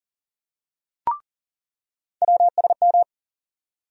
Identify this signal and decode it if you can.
Morse code, heard as a keyed tone, about 40 wpm, 700 Hz. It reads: WSM
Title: USA Call Sign Postfixes - 3 Letters 40wpm